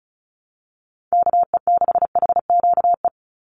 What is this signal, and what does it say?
Morse code, heard as a keyed tone, about 36 wpm, 700 Hz. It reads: KE6HQE
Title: USA Call Signs 35wpm